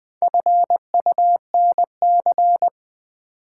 Morse code, heard as a keyed tone, about 20 wpm, 700 Hz. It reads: FUNC
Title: Build A Word - 4 Letters At A Time - 10-Letter Words - 20wpm